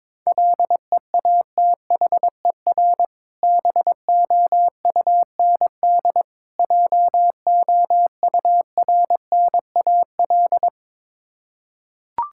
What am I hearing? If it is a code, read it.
Morse code, heard as a keyed tone, about 22 wpm, 690 Hz. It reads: LEATHER BOUND JOURNAL
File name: Difficult Words with Hints - 22wpm